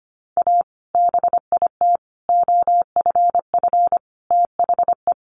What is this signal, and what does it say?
Morse code, heard as a keyed tone, about 25 wpm, 700 Hz. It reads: A BIT OFF THE
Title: Sentences from Top 900 Words 25wpm